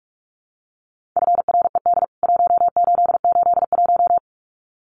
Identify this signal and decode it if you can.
Morse code, heard as a keyed tone, about 44 wpm, 700 Hz. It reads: FRED 1881
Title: CWT Exchange 45wpm